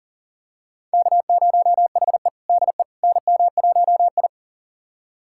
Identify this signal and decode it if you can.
Morse code, heard as a keyed tone, about 40 wpm, 700 Hz. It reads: K0HE DE NM1I
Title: USA Callsign DE Callsign - 4 Characters - 40wpm